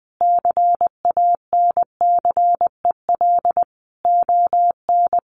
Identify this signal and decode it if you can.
Morse code, heard as a keyed tone, about 19 wpm, 700 Hz. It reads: CANCEL ON
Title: Sentences from Top 1200 Words 20wpm